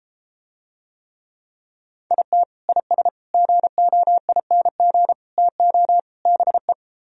Morse code, heard as a keyed tone, about 33 wpm, 700 Hz. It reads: IT IS GOING TO BE